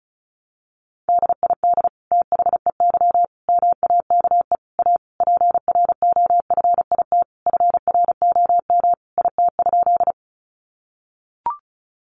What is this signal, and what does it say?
Morse code, heard as a keyed tone, about 34 wpm, 700 Hz. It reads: DID THEY MAKE A PROFIT FROM IT?
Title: Sentences from Top 800 Words 35wpm